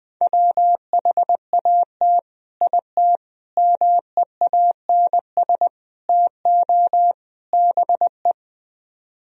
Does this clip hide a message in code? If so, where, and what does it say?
Yes: Morse code from 0.2 to 8.3 s, heard as a keyed tone, about 20 wpm, 700 Hz. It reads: WHAT IT MEANS TO BE